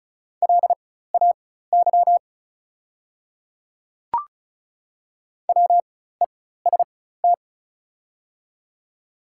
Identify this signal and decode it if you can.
Morse code, heard as a keyed tone, about 36 wpm, spaced as for 16 wpm, 700 Hz. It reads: LAY WEST
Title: 4x Character Spacing - Top 500 Words - 35wpm